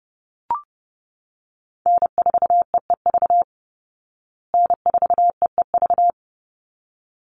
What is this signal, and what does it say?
Morse code, heard as a keyed tone, about 30 wpm, 700 Hz. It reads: N4EEV N4EEV
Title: Call Signs Repeated 3x - 30wpm